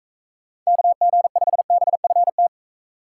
Morse code, heard as a keyed tone, about 44 wpm, 700 Hz. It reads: KG5BFT